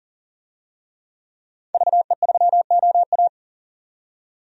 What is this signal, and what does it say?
Morse code, heard as a keyed tone, about 39 wpm, 700 Hz. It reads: VE3OA
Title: DXCC-100 Call Signs - 5 Characters - 40wpm